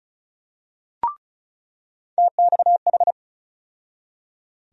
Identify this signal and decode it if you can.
Morse code, heard as a keyed tone, about 34 wpm, 700 Hz. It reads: TXH
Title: USA Call Sign Postfixes - 3 Letters 35wpm